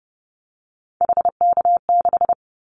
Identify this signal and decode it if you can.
Morse code, heard as a keyed tone, about 30 wpm, 700 Hz. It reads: HK6
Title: Callsign Prefixes for Top 100 DXCC - 30wpm